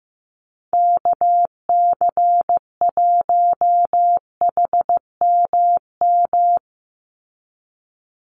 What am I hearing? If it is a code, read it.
Morse code, heard as a keyed tone, about 15 wpm, 700 Hz. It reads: KC1HMM
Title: USA Call Signs 15wpm